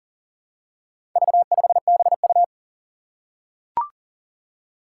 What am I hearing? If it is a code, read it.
Morse code, heard as a keyed tone, about 40 wpm, 700 Hz. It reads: V5BU
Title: Four Challenging Letter-Number Combinations 40wpm